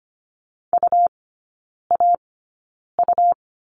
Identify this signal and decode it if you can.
Morse code, heard as a keyed tone, about 25 wpm, spaced as for 9 wpm, 700 Hz. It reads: UAU
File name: Lesson 17e - Letter u - Three Characters 25wpm